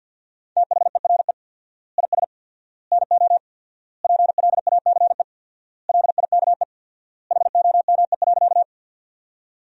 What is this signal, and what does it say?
Morse code, heard as a keyed tone, about 49 wpm, 700 Hz. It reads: THERE IS NO PLACE LIKE HOME.
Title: 4x Word-Spacing - Sentences from Top 200 Words - 50wpm